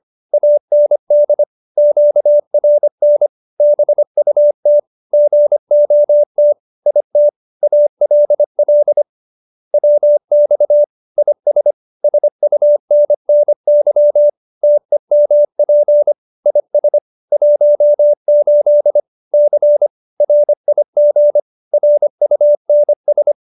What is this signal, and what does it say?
Morse code, heard as a keyed tone, about 25 wpm, 590 Hz. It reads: AND QRN BUT GOT IT ALL WX IS SUNNY TEMP IS 18 C RIG RUNS
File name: QSO310 25